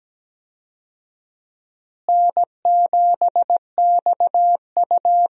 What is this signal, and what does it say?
Morse code, heard as a keyed tone, about 17 wpm, 700 Hz. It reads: N7XU
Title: USA Call Signs - 4 Characters - Repeated 3x - 17wpm